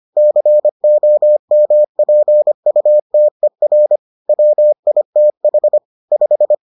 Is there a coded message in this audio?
Yes: Morse code from 0.2 to 6.5 s, heard as a keyed tone, about 25 wpm, 600 Hz. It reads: COMPUTER WITH 5